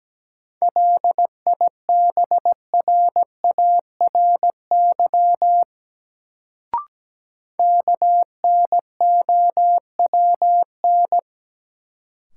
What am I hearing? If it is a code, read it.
Morse code, heard as a keyed tone, about 17 wpm, 700 Hz. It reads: LIBRARY KNOWN
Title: Sentences from Top 600 Words - New Word Drill - 17wpm